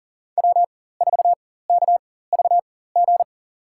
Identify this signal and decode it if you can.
Morse code, heard as a keyed tone, about 40 wpm, spaced as for 19 wpm, 700 Hz. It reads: W4KVG
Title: USA Call Signs - 5 Characters - 4x Character Spacing - 40wpm